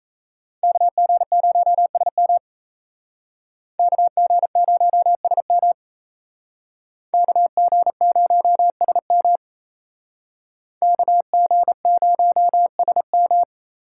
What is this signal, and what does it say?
Morse code, heard as a keyed tone, about 36 wpm, 700 Hz. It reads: KG0SM KG0SM KG0SM KG0SM